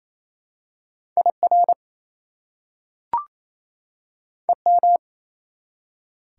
Morse code, heard as a keyed tone, about 28 wpm, 700 Hz. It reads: IR EM
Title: Prefixes and Suffixes 28wpm